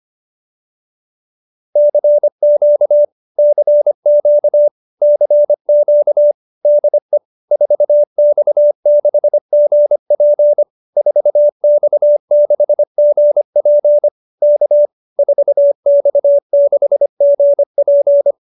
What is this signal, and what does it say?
Morse code, heard as a keyed tone, about 25 wpm, 590 Hz. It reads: CQ CQ CQ DE 4X6GP 4X6GP K 4X6GP
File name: QSO315 25